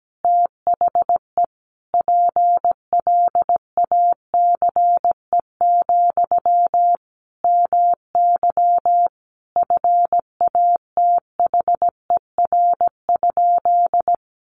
Morse code, heard as a keyed tone, about 17 wpm, 700 Hz. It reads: THE PLACE, MY FATHER?
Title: Sentences from Top 400 Words 17wpm